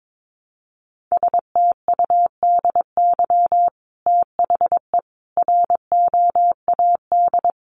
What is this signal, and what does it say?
Morse code, heard as a keyed tone, about 22 wpm, 700 Hz. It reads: STUDY THE ROAD
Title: Sentences from Top 1300 Words 22wpm